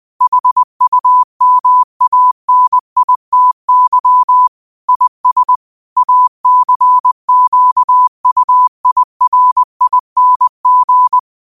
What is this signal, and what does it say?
Morse code, heard as a keyed tone, about 20 wpm, 1000 Hz. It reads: HUMANITY IS ACQUIRING